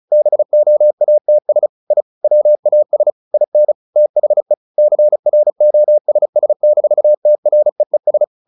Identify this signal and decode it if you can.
Morse code, heard as a keyed tone, about 34 wpm, 600 Hz. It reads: BOATS I WAS IN THE CROSS-TREES
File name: the old man and the sea 2 35wpm